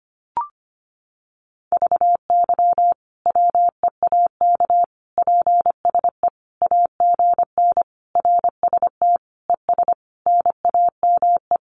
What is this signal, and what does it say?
Morse code, heard as a keyed tone, about 25 wpm, 700 Hz. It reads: VY WEAK PSE AGN RST ES NAME
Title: Common QSO Phrases 25wpm